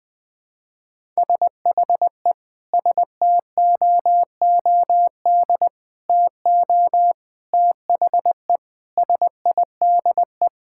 Morse code, heard as a keyed tone, about 21 wpm, 700 Hz. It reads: SHE STOOD TO THE SIDE
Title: Sentences with 5-Word Phrases - 20wpm